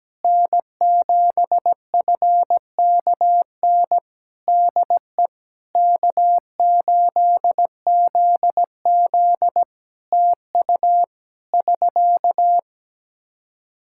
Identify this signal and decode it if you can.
Morse code, heard as a keyed tone, about 17 wpm, 700 Hz. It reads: N7FKN DE K8ZZ TU <SK>
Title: Practice QSO - Exchange 4 - Ending - 17wpm